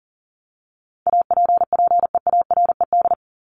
Morse code, heard as a keyed tone, about 40 wpm, 700 Hz. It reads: APPEARED
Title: ED Suffix - Top 100 Words with Base Words 40wpm